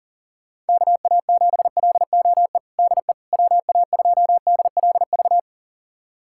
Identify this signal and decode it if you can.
Morse code, heard as a keyed tone, about 40 wpm, 700 Hz. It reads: KA7LOE DE WA2DLV
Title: USA Callsign DE Callsign - 6 Characters - 40wpm